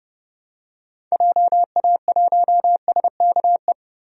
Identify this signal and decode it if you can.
Morse code, heard as a keyed tone, about 30 wpm, 700 Hz. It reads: JA1SKE
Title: DXCC-100 Call Signs - 6 Characters - 30wpm